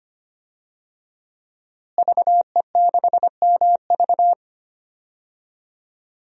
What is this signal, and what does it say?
Morse code, heard as a keyed tone, about 25 wpm, 700 Hz. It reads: VE6MV
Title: DXCC-100 Call Signs - 5 Characters - Repeated 3x - 25wpm